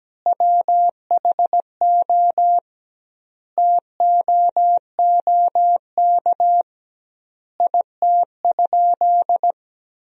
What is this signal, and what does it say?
Morse code, heard as a keyed tone, about 17 wpm, 700 Hz. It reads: WHO TOOK IT?
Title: 2x Word-Spacing - Sentences from Top 300 Words - 17wpm